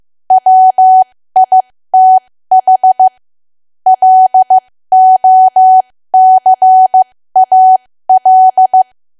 Morse code, heard as a keyed tone, about 15 wpm, spaced as for 13 wpm, 750 Hz. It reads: WITH LOCAL
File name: cwa / W1AW-Level2 / 13WPM